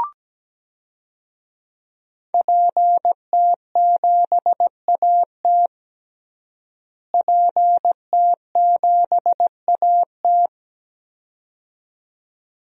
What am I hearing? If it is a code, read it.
Morse code, heard as a keyed tone, about 17 wpm, 700 Hz. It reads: PT7AT PT7AT
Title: DXCC-100 Call Signs - 5 Characters - Repeated 2x - 17wpm